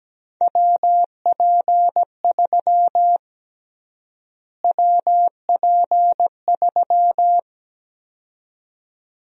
Morse code, heard as a keyed tone, about 17 wpm, 700 Hz. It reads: WP3 WP3
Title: USA Call Sign Prefixes - Repeated 2x - 17wpm